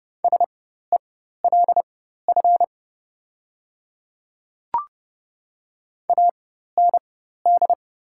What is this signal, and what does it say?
Morse code, heard as a keyed tone, about 30 wpm, spaced as for 14 wpm, 700 Hz. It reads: SELF AND
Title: 4x Character Spacing - Top 200 Words - 30wpm